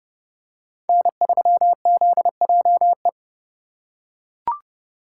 Morse code, heard as a keyed tone, about 32 wpm, 700 Hz. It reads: N3ZJE
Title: USA Call Signs - 5 Characters - 4x Character Spacing - 30wpm